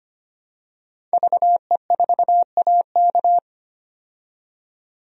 Morse code, heard as a keyed tone, about 25 wpm, 700 Hz. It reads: VE4AK
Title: DXCC-100 Call Signs - 5 Characters - 25wpm